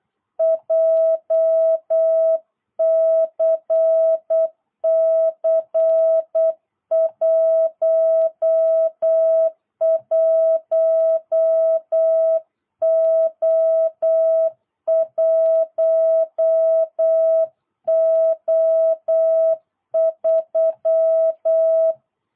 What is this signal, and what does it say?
Morse code, heard as a keyed tone, about 8 wpm, 660 Hz. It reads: JCC11O1O3
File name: jh8cbh / 2022cw